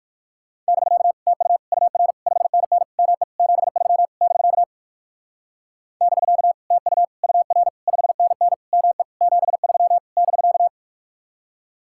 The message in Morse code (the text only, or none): <BK> TU UR 5NN ME 73 <BK> <BK> TU UR 5NN ME 73 <BK>